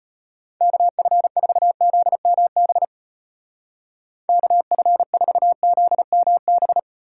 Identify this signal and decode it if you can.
Morse code, heard as a keyed tone, about 34 wpm, 700 Hz. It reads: KF4ZMB KF4ZMB